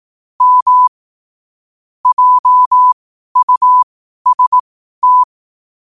Morse code, heard as a keyed tone, about 18 wpm, spaced as for 12 wpm, 1000 Hz. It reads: M JUST
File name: qotdfiles / 12wpm